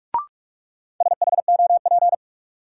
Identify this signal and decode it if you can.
Morse code, heard as a keyed tone, about 44 wpm, 700 Hz. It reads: SHOP
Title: Sentences from Top 1600 Words - New Word Drill - 45wpm